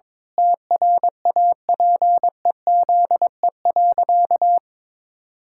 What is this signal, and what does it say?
Morse code, heard as a keyed tone, about 22 wpm, 700 Hz. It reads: TRAPEZE.
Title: Pangram Challenge v2 - 22wpm